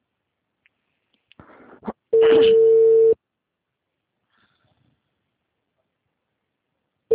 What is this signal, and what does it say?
Ringback tone (China, 450 Hz).